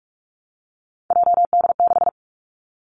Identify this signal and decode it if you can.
Morse code, heard as a keyed tone, about 46 wpm, 700 Hz. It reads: JD6
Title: Three Challenging Letter-Number Combinations 45wpm